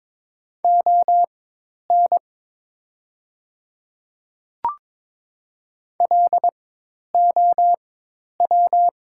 Morse code, heard as a keyed tone, about 22 wpm, spaced as for 10 wpm, 690 Hz. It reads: ON LOW
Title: 4x Character Spacing - Top 300 Words - 22wpm